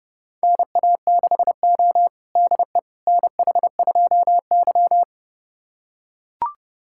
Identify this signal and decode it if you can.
Morse code, heard as a keyed tone, about 30 wpm, 700 Hz. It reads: NA6O DE NH2Y
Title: USA Callsign DE Callsign - 4 Characters - 30wpm